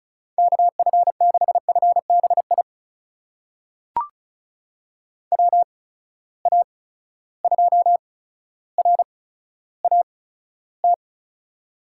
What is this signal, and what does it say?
Morse code, heard as a keyed tone, about 36 wpm, spaced as for 15 wpm, 700 Hz. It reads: KF6FBI WA2RAT
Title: USA Call Signs - 6 Characters - 8x Character Spacing - 35wpm